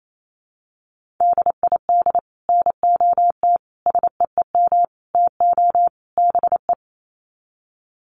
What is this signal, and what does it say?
Morse code, heard as a keyed tone, about 28 wpm, 700 Hz. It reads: DID NOT SEEM TO BE
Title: Sets of 5 Words Encore 28wpm